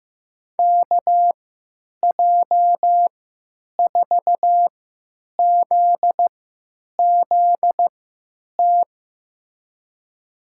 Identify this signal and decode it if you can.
Morse code, heard as a keyed tone, about 15 wpm, spaced as for 9 wpm, 700 Hz. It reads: KJ4ZZT
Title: USA Call Signs - 6 Characters - 3x Character Spacing - 15wpm